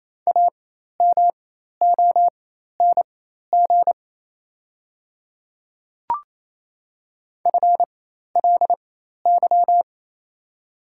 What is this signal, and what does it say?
Morse code, heard as a keyed tone, about 27 wpm, spaced as for 13 wpm, 700 Hz. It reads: AMONG FLY
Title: 4x Character Spacing - Top 500 Words - 28wpm